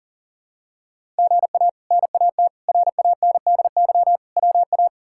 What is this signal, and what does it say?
Morse code, heard as a keyed tone, about 40 wpm, 700 Hz. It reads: GA NAT RANDY WA
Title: K1USN SST QSO Practice - 40wpm